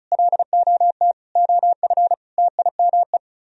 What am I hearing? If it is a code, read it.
Morse code, heard as a keyed tone, about 37 wpm, 700 Hz. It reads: LOT OF TIME
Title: Speed-Racing - Sets of 4 Words 30wpm